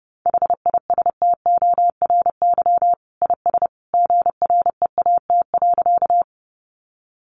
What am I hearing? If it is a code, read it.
Morse code, heard as a keyed tone, about 30 wpm, 700 Hz. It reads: HISTORY IS GREAT.
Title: Sentences from Top 600 Words 30wpm